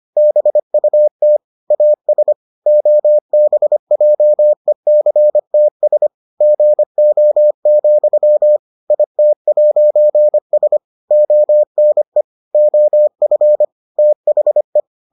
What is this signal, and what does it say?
Morse code, heard as a keyed tone, about 25 wpm, 600 Hz. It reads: BUT AS OBJECTS GO, IT'S ONE OF THE